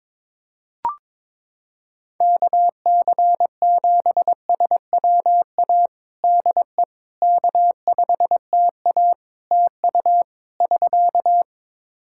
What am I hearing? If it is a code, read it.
Morse code, heard as a keyed tone, about 22 wpm, 700 Hz. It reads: KC7SWA DE K5TA TU <SK>
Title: Practice QSO - Exchange 4 - Ending - 22wpm